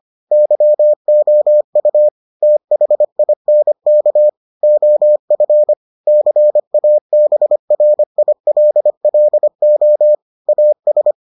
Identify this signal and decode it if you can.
Morse code, heard as a keyed tone, about 25 wpm, 600 Hz. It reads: YOU THINK OF CABRILLO AS